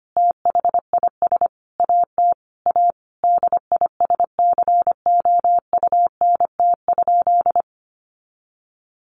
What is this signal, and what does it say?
Morse code, heard as a keyed tone, about 25 wpm, 700 Hz. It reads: THIS AT A DISCOUNT?
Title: Sentences from Top 1200 Words 25wpm